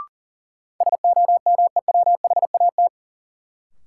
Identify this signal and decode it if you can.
Morse code, heard as a keyed tone, about 40 wpm, 700 Hz. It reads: SOMEWHAT
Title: Sentences from Top 1300 Words - New Word Drill Alt - 40wpm